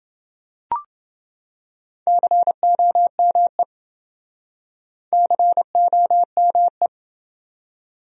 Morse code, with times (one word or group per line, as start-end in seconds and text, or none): 2.1-3.6 COME
5.1-6.9 COME